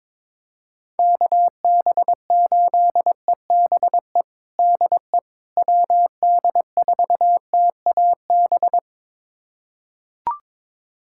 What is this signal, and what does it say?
Morse code, heard as a keyed tone, about 23 wpm, 700 Hz. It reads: KB8EBE DE WD4TAB
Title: USA Callsign DE Callsign - 6 Characters - 22wpm